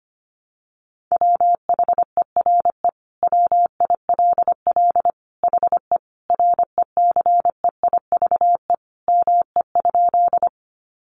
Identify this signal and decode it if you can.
Morse code, heard as a keyed tone, about 25 wpm, 700 Hz. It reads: WHERE WILL HE RECEIVE ME?
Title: Sentences from Top 1000 Words 25wpm